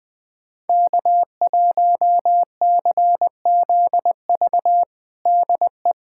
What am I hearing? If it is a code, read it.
Morse code, heard as a keyed tone, about 21 wpm, 700 Hz. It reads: K1CZV DE
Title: USA Callsign DE Callsign - 5 Characters - 20wpm